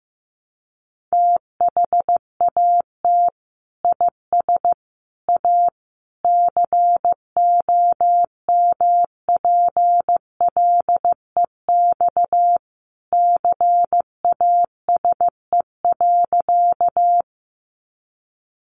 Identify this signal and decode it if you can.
Morse code, heard as a keyed tone, about 15 wpm, 700 Hz. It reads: THAT IS A COMPLEX CASE.
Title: Sentences from Top 900 Words 15wpm